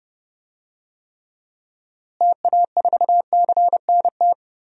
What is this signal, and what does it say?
Morse code, heard as a keyed tone, about 30 wpm, 700 Hz. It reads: TA4CNT
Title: DXCC-100 Call Signs - 6 Characters - Repeated 2x - 30wpm